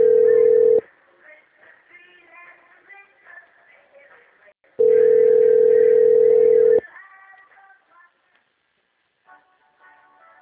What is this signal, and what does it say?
Ringback tone (North America), 440 + 480 Hz.